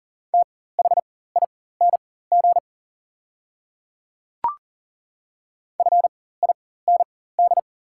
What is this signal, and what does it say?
Morse code, heard as a keyed tone, about 40 wpm, spaced as for 19 wpm, 690 Hz. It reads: THING FIND
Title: 4x Character Spacing - Top 100 Words - 40wpm